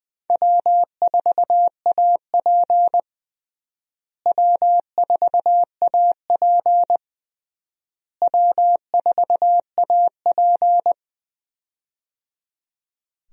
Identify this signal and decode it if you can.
Morse code, heard as a keyed tone, about 21 wpm, 700 Hz. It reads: W4AP W4AP W4AP
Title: USA Call Signs - 4 Characters - Repeated 3x - 20wpm